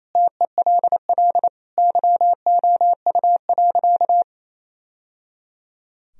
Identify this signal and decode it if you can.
Morse code, heard as a keyed tone, about 27 wpm, 700 Hz. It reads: TELL YOU.
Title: Sentences from Top 1000 Words - Review Part 3 - 28wpm